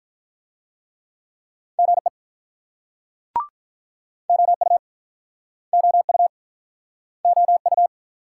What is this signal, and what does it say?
Morse code, heard as a keyed tone, about 50 wpm, 690 Hz. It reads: GE OU OU OU